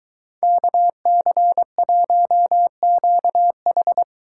Morse code, heard as a keyed tone, about 23 wpm, 700 Hz. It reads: KC1QH